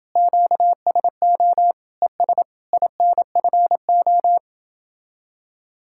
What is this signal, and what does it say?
Morse code, heard as a keyed tone, about 30 wpm, 700 Hz. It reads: QSO ES INFO